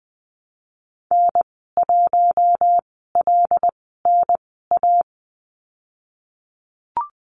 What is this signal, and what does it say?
Morse code, heard as a keyed tone, about 20 wpm, 700 Hz. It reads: N 1 L N A
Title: 076 Lesson 13g - Letter L - Call Signs 20wpm